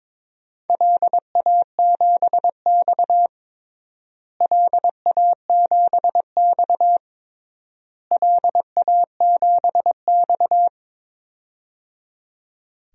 Morse code, heard as a keyed tone, about 22 wpm, 700 Hz. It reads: LA7X LA7X LA7X
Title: DXCC-100 Call Signs - 4 Characters - Repeated 3x - 22wpm